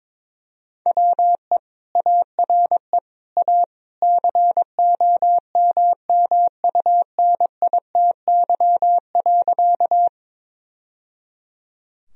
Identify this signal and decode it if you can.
Morse code, heard as a keyed tone, about 22 wpm, 700 Hz. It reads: WE ARE A COMMUNITY.
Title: Sentences from Top 1000 Words - Review Part 4 - 22wpm